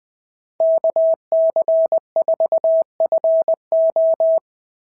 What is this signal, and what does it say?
Morse code, heard as a keyed tone, about 20 wpm, 660 Hz. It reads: KC4FO
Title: SOTA QSO Practice 20wpm